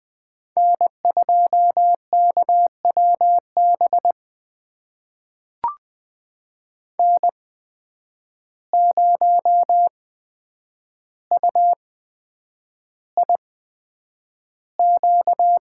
Morse code, heard as a keyed tone, about 20 wpm, spaced as for 9 wpm, 700 Hz. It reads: N2KWB N0UIQ
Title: USA Call Signs - 5 Characters - 8x Character Spacing - 20wpm